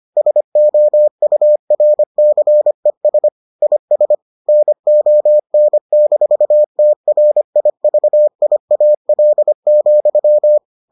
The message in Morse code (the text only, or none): SOURCES IS NON-TRIVIAL,